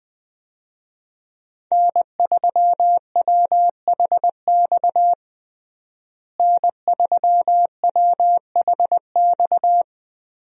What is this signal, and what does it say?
Morse code, heard as a keyed tone, about 20 wpm, 700 Hz. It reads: N3WHX N3WHX
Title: USA Call Signs - 5 Characters - Repeated 3x - 20wpm